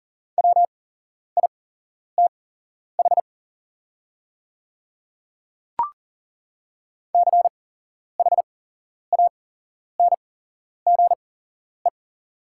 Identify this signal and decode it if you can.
Morse code, heard as a keyed tone, about 40 wpm, spaced as for 11 wpm, 690 Hz. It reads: WITH CHANGE
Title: 8x Character Spacing - Top 200 Words - 40wpm